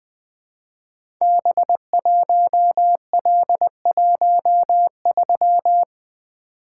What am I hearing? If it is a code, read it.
Morse code, heard as a keyed tone, about 20 wpm, 700 Hz. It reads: B1L13
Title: Code Groups - 5 Letter:Numbers - 20wpm